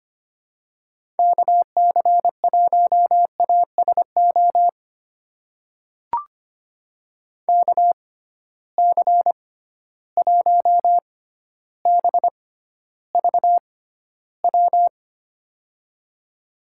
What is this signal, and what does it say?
Morse code, heard as a keyed tone, about 25 wpm, spaced as for 13 wpm, 700 Hz. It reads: KC1ASO KC1BVW
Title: USA Call Signs - 6 Characters - 6x Character Spacing - 25wpm